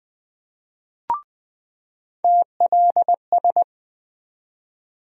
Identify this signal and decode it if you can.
Morse code, heard as a keyed tone, about 20 wpm, 700 Hz. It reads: TLS
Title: USA Call Sign Postfixes - 3 Letters 20wpm